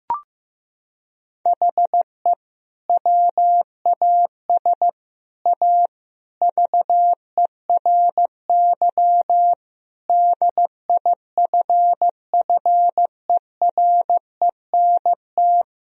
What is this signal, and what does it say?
Morse code, heard as a keyed tone, about 15 wpm, 700 Hz. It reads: HE WAS A VERY DIFFERENT
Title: Sentences from Top 1000 Words - Review Part 3 - 15wpm